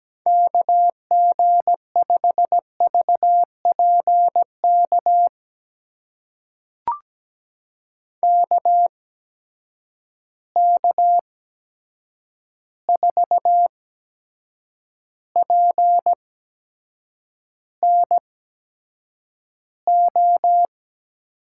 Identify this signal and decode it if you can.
Morse code, heard as a keyed tone, about 17 wpm, spaced as for 7 wpm, 700 Hz. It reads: KG5VPK KK4PNO
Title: USA Call Signs - 6 Characters - 8x Character Spacing - 17wpm